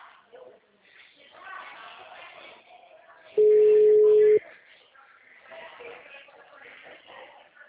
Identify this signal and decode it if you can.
Ringback tone (Europe, 425 Hz), with speech.